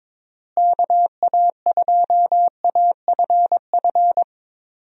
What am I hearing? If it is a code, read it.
Morse code, heard as a keyed tone, about 22 wpm, 700 Hz. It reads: KA2AFF
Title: USA Call Signs - 6 Characters - Repeated 3x - 22wpm